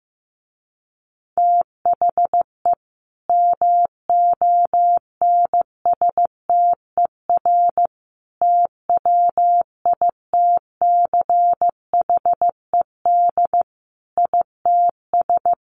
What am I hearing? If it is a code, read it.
Morse code, heard as a keyed tone, about 15 wpm, 700 Hz. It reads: THE MONSTER TWITCHED ITS
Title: A Sound Of Thunder 15wpm